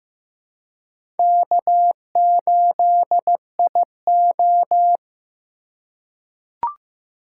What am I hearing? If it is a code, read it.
Morse code, heard as a keyed tone, about 15 wpm, 700 Hz. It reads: K8IO
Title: USA Call Signs - 4 Characters - 4x Character Spacing - 15wpm